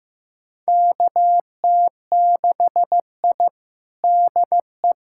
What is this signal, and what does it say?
Morse code, heard as a keyed tone, about 15 wpm, 700 Hz. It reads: KT6I DE
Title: USA Callsign DE Callsign - 4 Characters - 15wpm